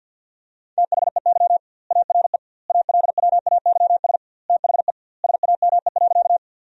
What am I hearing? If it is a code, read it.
Morse code, heard as a keyed tone, about 50 wpm, 700 Hz. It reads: THEY ARE ALWAYS THE SAME.
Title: 2x Word-Spacing - Sentences from Top 300 Words - 50wpm